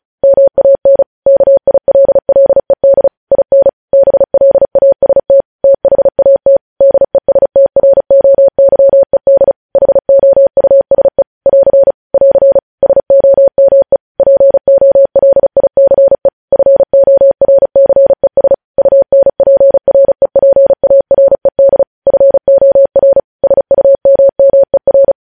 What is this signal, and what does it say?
Morse code, heard as a keyed tone, about 36 wpm, 550 Hz. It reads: MAN KILLED IN BLAST THAT DESTROYED HOUSE <AR> <AR> SOME POLICE FORCES UNPREPARED FOR SUMMER
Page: news headlines in morse code at 35 wpm